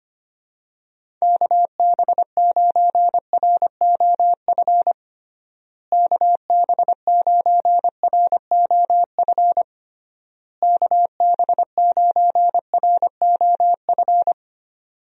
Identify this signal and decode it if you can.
Morse code, heard as a keyed tone, about 25 wpm, 700 Hz. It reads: KB9ROF KB9ROF KB9ROF
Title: USA Call Signs - 6 Characters - Repeated 3x - 25wpm